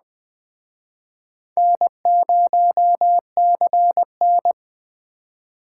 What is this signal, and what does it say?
Morse code, heard as a keyed tone, about 24 wpm, 700 Hz. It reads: N0CN
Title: Speed Racing - USA Call Signs 20wpm